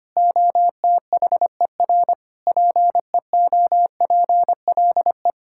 Morse code, heard as a keyed tone, about 25 wpm, 700 Hz. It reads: OTHER PEOPLE
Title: Sentences from Top 1000 Words - Review Part 6 - 25wpm